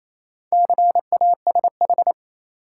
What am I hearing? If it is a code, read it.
Morse code, heard as a keyed tone, about 28 wpm, 700 Hz. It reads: CASH
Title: Sentences from Top 700 Words - New Word Drill - 28wpm